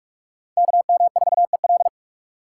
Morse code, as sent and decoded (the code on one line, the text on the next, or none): -.- -- ....- . .-..
KM4EL